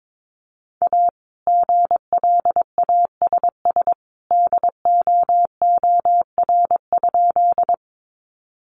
Morse code, heard as a keyed tone, about 22 wpm, 700 Hz. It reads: A GLASS DOOR?
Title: Sentences from Top 1300 Words 22wpm